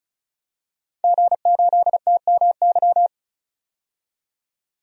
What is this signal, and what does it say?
Morse code, heard as a keyed tone, about 34 wpm, 700 Hz. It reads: G8TMY
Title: DXCC-100 Call Signs - 5 Characters - 35wpm